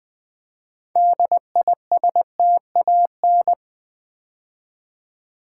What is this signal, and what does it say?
Morse code, heard as a keyed tone, about 20 wpm, 700 Hz. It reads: DISTAN
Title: Build A Word - 2 Letters At A Time - 8-Letter Words - 20wpm